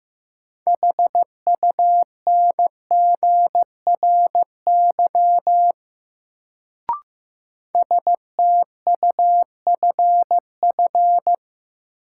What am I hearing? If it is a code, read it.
Morse code, heard as a keyed tone, about 16 wpm, 700 Hz. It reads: HUNGRY STUFF
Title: Sentences from Top 1400 Words - New Word Drill - 15wpm